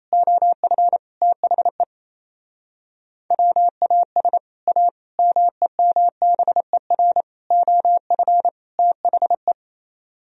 Morse code, heard as a keyed tone, about 37 wpm, 700 Hz. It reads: OF THE WAS A MEMBER OF THE